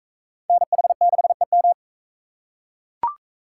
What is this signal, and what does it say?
Morse code, heard as a keyed tone, about 40 wpm, 700 Hz. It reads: NH6EM